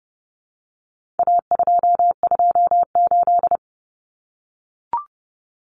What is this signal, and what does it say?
Morse code, heard as a keyed tone, about 30 wpm, 700 Hz. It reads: A228
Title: Common Cut Numbers - 4-Digit 30wpm